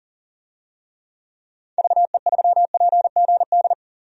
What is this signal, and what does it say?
Morse code, heard as a keyed tone, about 40 wpm, 700 Hz. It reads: VE3PGD
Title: DXCC-100 Call Signs - 6 Characters - 40wpm